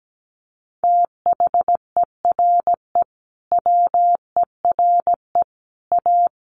Morse code, heard as a keyed tone, about 17 wpm, 700 Hz. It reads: THERE WERE A
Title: Sets of 5 Words 17wpm